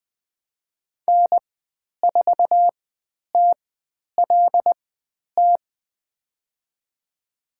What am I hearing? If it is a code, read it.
Morse code, heard as a keyed tone, about 20 wpm, spaced as for 10 wpm, 700 Hz. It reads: N4TLT
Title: 102 Lesson 17g - Letter u - Call Signs 20wpm